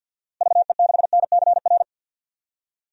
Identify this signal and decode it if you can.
Morse code, heard as a keyed tone, about 51 wpm, 700 Hz. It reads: VE6NCR
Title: DXCC-100 Call Signs - 6 Characters - Repeated 3x - 50wpm